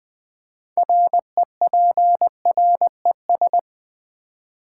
Morse code, heard as a keyed tone, about 20 wpm, 700 Hz. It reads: REPRES
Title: Build A Word - 3 Letters At A Time - 9-Letter Words - 20wpm